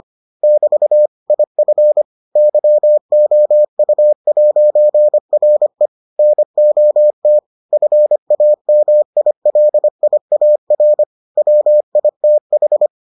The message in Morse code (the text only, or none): <BT> IF YOU'RE NOT FAMILIAR WITH